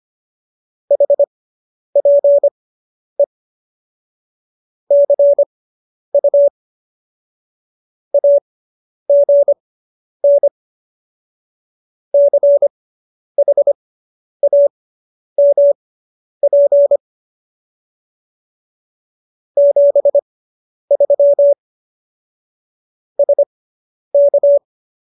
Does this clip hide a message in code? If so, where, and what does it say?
Yes: Morse code from 0.9 to 24.6 s, heard as a keyed tone, about 25 wpm, spaced as for 10 wpm, 570 Hz. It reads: HPE CU AGN CHAMP 73 SK